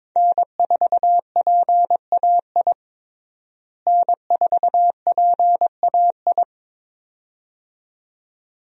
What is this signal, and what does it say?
Morse code, heard as a keyed tone, about 22 wpm, 700 Hz. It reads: N4PAI N4PAI
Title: USA Call Signs - 5 Characters - Repeated 2x - 22wpm